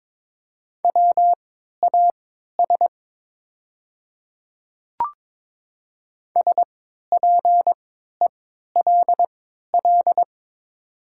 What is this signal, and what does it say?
Morse code, heard as a keyed tone, about 22 wpm, spaced as for 13 wpm, 700 Hz. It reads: WAS SPELL
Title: 3x Character Spacing - Top 200 Words - 22wpm